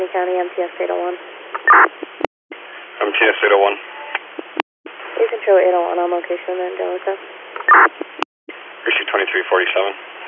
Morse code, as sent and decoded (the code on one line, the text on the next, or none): none